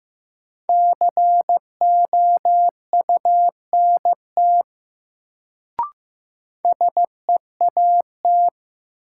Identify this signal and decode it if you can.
Morse code, heard as a keyed tone, about 15 wpm, 700 Hz. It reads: COUNT SEAT
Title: Sentences from Top 1500 Words - New Word Drill - 15wpm